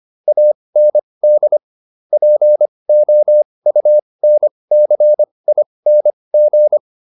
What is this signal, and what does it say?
Morse code, heard as a keyed tone, about 25 wpm, spaced as for 20 wpm, 600 Hz. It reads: AND POUNCING